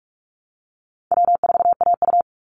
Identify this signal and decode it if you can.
Morse code, heard as a keyed tone, about 44 wpm, 700 Hz. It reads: W4AU
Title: Easy Call Signs Repeated 3x - 45wpm